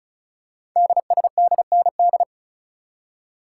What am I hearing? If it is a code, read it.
Morse code, heard as a keyed tone, about 36 wpm, 700 Hz. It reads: DSDND
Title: Code Groups - 5 Letters - 35wpm